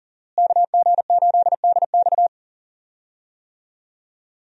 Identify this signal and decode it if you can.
Morse code, heard as a keyed tone, about 39 wpm, 700 Hz. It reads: KG8DX
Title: USA Call Signs - 5 Characters - Repeated 2x - 40wpm